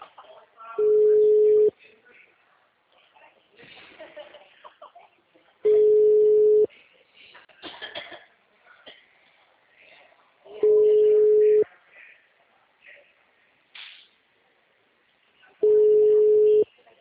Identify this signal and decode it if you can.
Ringback tone (Europe, 425 Hz), with speech.